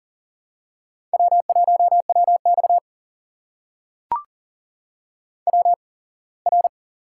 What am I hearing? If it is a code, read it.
Morse code, heard as a keyed tone, about 40 wpm, spaced as for 24 wpm, 700 Hz. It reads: W1WX WR
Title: USA Call Signs - 4 Characters - 8x Character Spacing - 40wpm